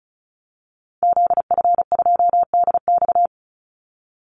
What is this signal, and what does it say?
Morse code, heard as a keyed tone, about 34 wpm, 700 Hz. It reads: ZF2DX
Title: Call Signs - 35wpm